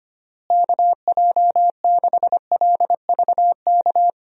Morse code, heard as a keyed tone, about 25 wpm, 700 Hz. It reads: KJ6LVK
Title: USA Call Signs - 6 Characters - Repeated 3x - 25wpm